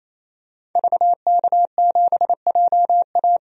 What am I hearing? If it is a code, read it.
Morse code, heard as a keyed tone, about 28 wpm, 700 Hz. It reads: VK7JA
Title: DXCC-100 Call Signs - 5 Characters - 28wpm